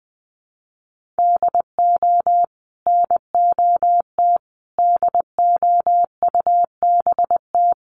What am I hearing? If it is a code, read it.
Morse code, heard as a keyed tone, about 20 wpm, 700 Hz. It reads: DO NOT DOUBT
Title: Sentences from Top 1500 Words 20wpm